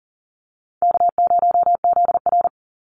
Morse code, heard as a keyed tone, about 40 wpm, 700 Hz. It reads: K0ZR
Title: Easy Call Signs - 40wpm